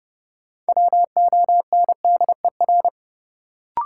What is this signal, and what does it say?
Morse code, heard as a keyed tone, about 30 wpm, 700 Hz. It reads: WONDER
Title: Sentences from Top 500 Words - New Word Drill - 30wpm